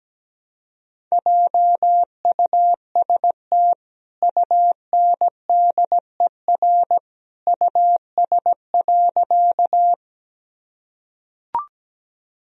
Morse code, heard as a keyed tone, about 17 wpm, 700 Hz. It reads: JUST UNDER US.
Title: Sentences from Top 1000 Words - Review Part 1 - 17wpm